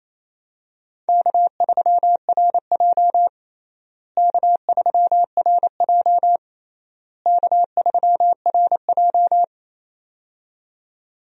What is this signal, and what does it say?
Morse code, heard as a keyed tone, about 28 wpm, 700 Hz. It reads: K3RJ K3RJ K3RJ
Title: USA Call Signs - 4 Characters - Repeated 3x - 28wpm